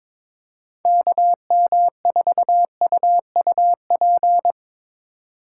Morse code, heard as a keyed tone, about 22 wpm, 700 Hz. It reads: KM4UUP
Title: USA Call Signs - 8x Character Spacing - 22wpm